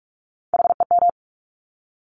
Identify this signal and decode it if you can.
Morse code, heard as a keyed tone, about 46 wpm, 690 Hz. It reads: HEM
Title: Top Three Letter Combinations in Words 45wpm